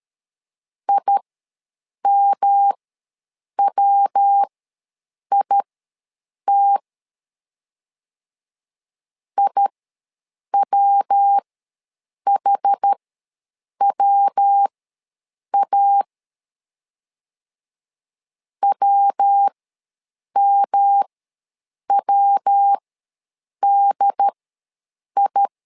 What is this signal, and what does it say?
Morse code, heard as a keyed tone, about 13 wpm, spaced as for 7 wpm, 800 Hz. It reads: IMWIT IWHWA WMWDI
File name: LEZIONI / 3 MWRO / 3 MW ws 12wpm 1816